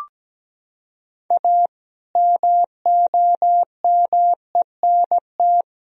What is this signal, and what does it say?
Morse code, heard as a keyed tone, about 17 wpm, 700 Hz. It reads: A MOMENT
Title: Sentences from Top 1000 Words - Review Part 4 - 17wpm